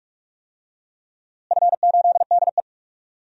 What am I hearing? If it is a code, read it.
Morse code, heard as a keyed tone, about 41 wpm, 700 Hz. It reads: F8DE
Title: DXCC-100 Call Signs - 4 Characters - Repeated 3x - 45wpm